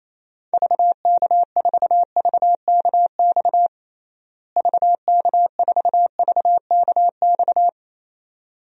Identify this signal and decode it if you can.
Morse code, heard as a keyed tone, about 28 wpm, 700 Hz. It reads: VK4VKX VK4VKX
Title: DXCC-100 Call Signs - 6 Characters - Repeated 2x - 28wpm